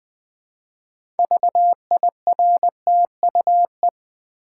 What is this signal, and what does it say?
Morse code, heard as a keyed tone, about 20 wpm, 700 Hz. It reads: VIRTUE
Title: Build A Word - 2 Letters At A Time - 6-Letter Words - 20wpm